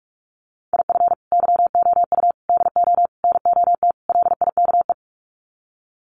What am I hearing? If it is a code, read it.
Morse code, heard as a keyed tone, about 44 wpm, 700 Hz. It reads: IF YOU DO NOT LIKE
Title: Sets of 5 Words 45wpm